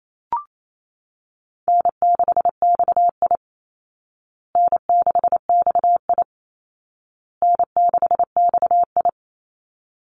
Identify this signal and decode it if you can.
Morse code, heard as a keyed tone, about 28 wpm, 700 Hz. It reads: N6XI N6XI N6XI
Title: Easy Call Signs Repeated 3x - 28wpm